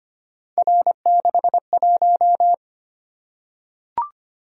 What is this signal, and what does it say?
Morse code, heard as a keyed tone, about 25 wpm, 700 Hz. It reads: R61
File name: Three Letter-Number Combinations - 25wpm